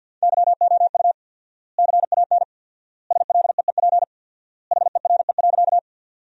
Morse code, heard as a keyed tone, about 53 wpm, 700 Hz. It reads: YOU CAN SLEEP HERE.
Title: 4x Word-Spacing - Sentences from Top 500 Words - 50wpm